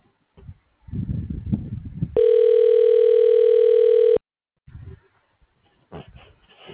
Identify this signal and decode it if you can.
Ringback tone (North America), 440 + 480 Hz.
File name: recordings / RE8a2e424f73bb c5255c1aa06180